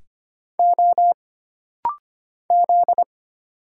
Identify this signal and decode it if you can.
Morse code, heard as a keyed tone, about 25 wpm, 700 Hz. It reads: O Z
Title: Single Letters - Rapid-Fire 25wpm